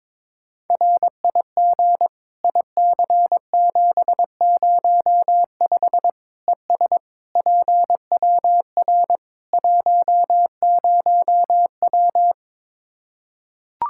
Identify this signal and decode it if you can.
Morse code, heard as a keyed tone, about 22 wpm, 700 Hz. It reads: RIG IC705 ES PWR 10W
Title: Practice QSO - Exchange 2 - Info - 22wpm